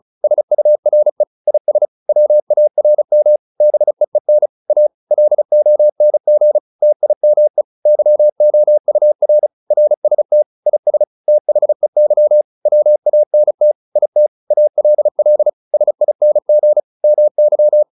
Morse code, heard as a keyed tone, about 34 wpm, 600 Hz. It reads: SURE IS WARM BEEN A LONG TIME YOUR RST IS THEY WANT IT ALL SING MY